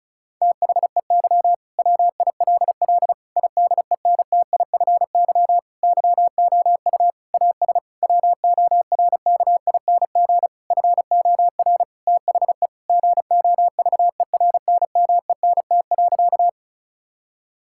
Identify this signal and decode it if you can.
Morse code, heard as a keyed tone, about 34 wpm, 700 Hz. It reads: THEY WILL IDENTIFY YOU AS WORKING FOR THE GOVERNMENT.
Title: Sentences from Top 1000 Words - Review Part 6 - 35wpm